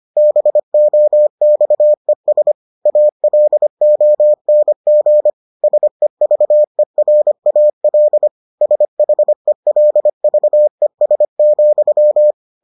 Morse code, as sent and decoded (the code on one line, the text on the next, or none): -... --- -..- . ... / .- .-.. --- -. --. / ... ...- . .-. .- .-.. / ... .... .-.. ...- . ... --..--
BOXES ALONG SEVERAL SHELVES,